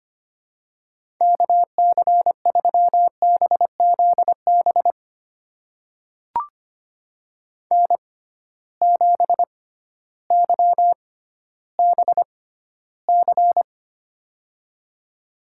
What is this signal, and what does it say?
Morse code, heard as a keyed tone, about 25 wpm, spaced as for 13 wpm, 700 Hz. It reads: KC3BZB N7YBC
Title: USA Call Signs - 6x Character Spacing - 25wpm